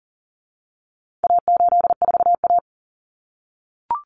Morse code, heard as a keyed tone, about 40 wpm, 700 Hz. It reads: A84A
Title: Common Cut Numbers - 4-Digit 40wpm